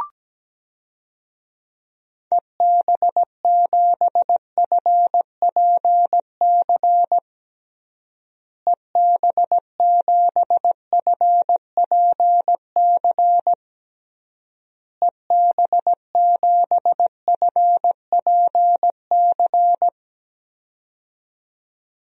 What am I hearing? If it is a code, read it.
Morse code, heard as a keyed tone, about 17 wpm, 700 Hz. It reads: EB7FPC EB7FPC EB7FPC
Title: DXCC-100 Call Signs - 6 Characters - Repeated 3x - 17wpm